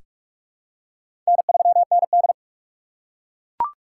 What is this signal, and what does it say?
Morse code, heard as a keyed tone, about 41 wpm, 700 Hz. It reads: N3ND
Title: K1USN SST QSO Practice - 45wpm